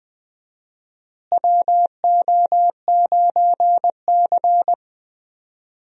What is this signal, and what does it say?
Morse code, heard as a keyed tone, about 20 wpm, 700 Hz. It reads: WO9C
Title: USA Call Signs - 4 Characters - Repeated 3x - 20wpm